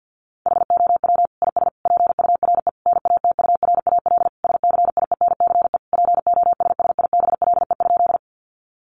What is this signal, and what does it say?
Morse code, heard as a keyed tone, about 50 wpm, 700 Hz. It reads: HOW IS PURE NATURAL SCIENCE POSSIBLE?
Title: Sentences from Top 1500 Words 50wpm